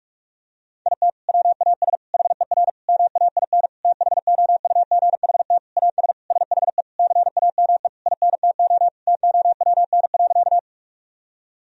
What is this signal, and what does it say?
Morse code, heard as a keyed tone, about 41 wpm, 700 Hz. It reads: IT WAS HER MAIN THOUGHT AS SHE CAME INTO TOWN.
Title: Sentences from Top 1000 Words - Review Part 2 - 45wpm